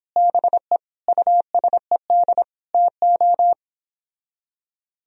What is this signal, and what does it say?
Morse code, heard as a keyed tone, about 26 wpm, 700 Hz. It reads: BE USED TO
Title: Speed-Racing - Sets of 4 Words 22wpm